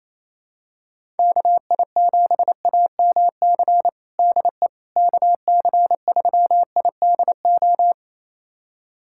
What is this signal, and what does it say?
Morse code, heard as a keyed tone, about 28 wpm, 700 Hz. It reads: KI7AMC DE KC3IDO
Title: USA Callsign DE Callsign 28wpm